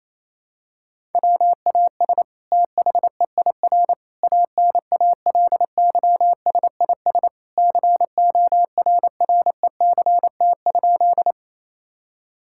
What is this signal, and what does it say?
Morse code, heard as a keyed tone, about 28 wpm, 700 Hz. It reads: WAS THEIR ANALYSIS CORRECT?
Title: Sentences from Top 1000 Words - Review Part 5 - 28wpm